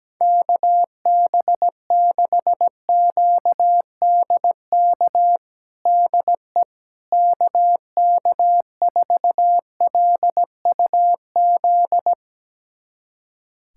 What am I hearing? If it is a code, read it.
Morse code, heard as a keyed tone, about 17 wpm, 700 Hz. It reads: KB6QDK DE KK4LUZ
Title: USA Callsign DE Callsign 17wpm